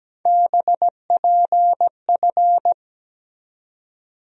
Morse code, heard as a keyed tone, about 17 wpm, 700 Hz. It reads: BPF
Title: USA Call Sign Postfixes - 3 Letters 17wpm